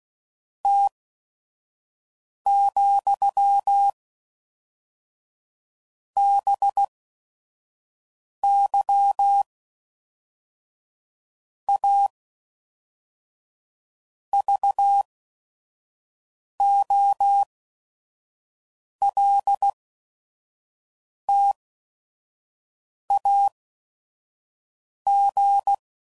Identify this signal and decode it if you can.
Morse code, heard as a keyed tone, about 16 wpm, 800 Hz. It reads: T , B Y A V O L T A G